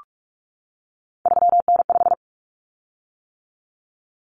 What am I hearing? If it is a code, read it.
Morse code, heard as a keyed tone, about 44 wpm, 700 Hz. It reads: 3N5
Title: Common Cut Numbers - 3-Digit 45wpm